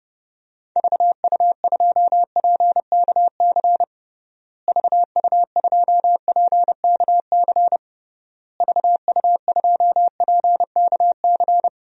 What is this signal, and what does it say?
Morse code, heard as a keyed tone, about 32 wpm, 700 Hz. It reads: VU2PKC VU2PKC VU2PKC
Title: DXCC-100 Call Signs - 6 Characters - Repeated 3x - 30wpm